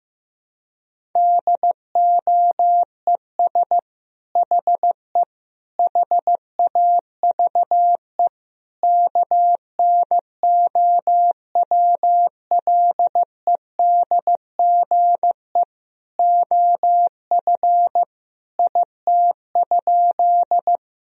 Morse code, heard as a keyed tone, about 15 wpm, 700 Hz. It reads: DOES HE HAVE KNOWLEDGE OF IT?
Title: Sentences from Top 1000 Words - Review Part 3 - 15wpm